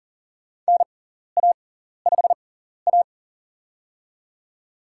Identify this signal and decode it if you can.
Morse code, heard as a keyed tone, about 40 wpm, spaced as for 14 wpm, 700 Hz. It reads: NA5A
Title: USA Call Signs - 4 Characters - 6x Character Spacing - 40wpm